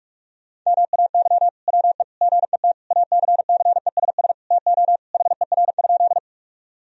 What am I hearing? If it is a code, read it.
Morse code, heard as a keyed tone, about 44 wpm, 700 Hz. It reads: MAY WE GET ACCESS TO HER?
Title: Sentences from Top 1000 Words - Review Part 4 - 45wpm